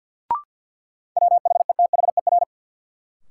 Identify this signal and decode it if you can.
Morse code, heard as a keyed tone, about 50 wpm, 700 Hz. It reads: WHETHER
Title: Sentences from Top 1300 Words - New Word Drill Alt - 50wpm